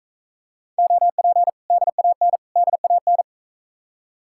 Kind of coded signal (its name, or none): Morse code